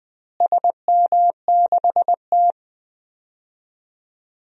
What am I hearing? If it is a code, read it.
Morse code, heard as a keyed tone, about 20 wpm, 700 Hz. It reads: SM6T
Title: DXCC-100 Call Signs - 4 Characters - Repeated 3x - 20wpm